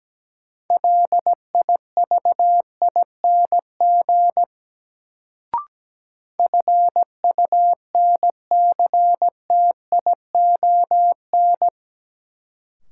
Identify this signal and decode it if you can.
Morse code, heard as a keyed tone, about 17 wpm, 700 Hz. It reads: LIVING FUNCTION
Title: Sentences from Top 1100 Words - New Word Drill Alt - 17wpm